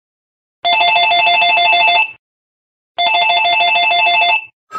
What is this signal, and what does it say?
There is no code.